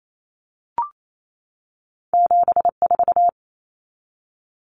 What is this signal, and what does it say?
Morse code, heard as a keyed tone, about 28 wpm, 700 Hz. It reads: 7 4
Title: Common Cut Numbers - 2-Digit 28wpm